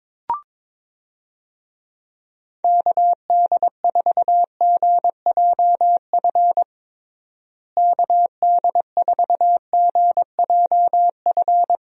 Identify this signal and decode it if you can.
Morse code, heard as a keyed tone, about 22 wpm, 700 Hz. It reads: KD4GJF KD4GJF
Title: USA Call Signs - 6 Characters - Repeated 3x - 22wpm